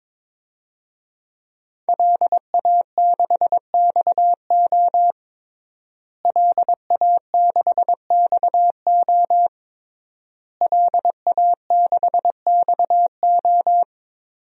Morse code, heard as a keyed tone, about 22 wpm, 700 Hz. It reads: LA6XO LA6XO LA6XO
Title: DXCC-100 Call Signs - 5 Characters - Repeated 3x - 22wpm